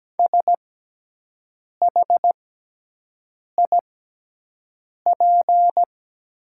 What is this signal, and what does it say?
Morse code, heard as a keyed tone, about 17 wpm, spaced as for 6 wpm, 700 Hz. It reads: SHIP
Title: 6x Character Spacing - Top 500 Words - 17wpm